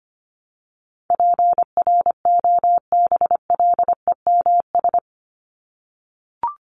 Morse code, heard as a keyed tone, about 25 wpm, 700 Hz. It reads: PROBLEMS
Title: ES:S Suffix - Top 100 Words with Base Words 25wpm